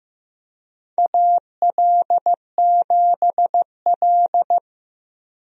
Morse code, heard as a keyed tone, about 16 wpm, 700 Hz. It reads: AL7L
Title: USA Call Signs - 4 Characters - Repeated 3x - 15wpm